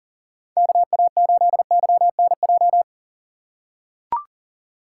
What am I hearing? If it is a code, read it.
Morse code, heard as a keyed tone, about 37 wpm, 700 Hz. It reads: KA8YNJ
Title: USA Call Signs - 4x Character Spacing - 40wpm